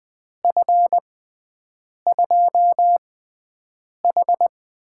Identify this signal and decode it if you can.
Morse code, heard as a keyed tone, about 20 wpm, spaced as for 7 wpm, 700 Hz. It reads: F2H
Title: USA Call Signs - 6x Character Spacing - 20wpm